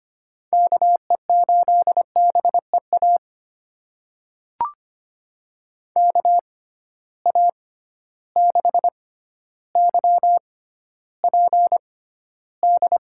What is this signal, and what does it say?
Morse code, heard as a keyed tone, about 25 wpm, spaced as for 13 wpm, 700 Hz. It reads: KE8BEA KA6YPD
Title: USA Call Signs - 6 Characters - 6x Character Spacing - 25wpm